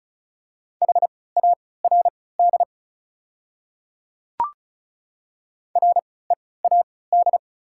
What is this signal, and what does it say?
Morse code, heard as a keyed tone, about 36 wpm, spaced as for 20 wpm, 700 Hz. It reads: HARD READ
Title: 3x Character Spacing - Top 500 Words - 35wpm